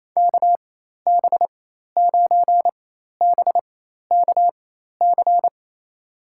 Morse code, heard as a keyed tone, about 28 wpm, spaced as for 13 wpm, 700 Hz. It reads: KB9BKC
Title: USA Call Signs - 6 Characters - 4x Character Spacing - 28wpm